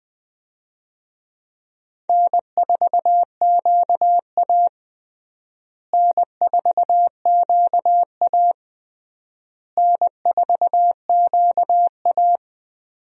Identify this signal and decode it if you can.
Morse code, heard as a keyed tone, about 21 wpm, 700 Hz. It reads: N4QA N4QA N4QA
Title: USA Call Signs - 4 Characters - Repeated 3x - 20wpm